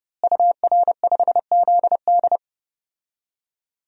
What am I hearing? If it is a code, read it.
Morse code, heard as a keyed tone, about 31 wpm, 700 Hz. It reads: UR5ZD
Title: DXCC-100 Call Signs - 5 Characters - Repeated 2x - 30wpm